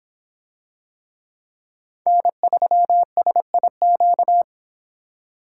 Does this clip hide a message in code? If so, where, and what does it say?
Yes: Morse code from 2.1 to 4.4 s, heard as a keyed tone, about 25 wpm, 700 Hz. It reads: N3SIQ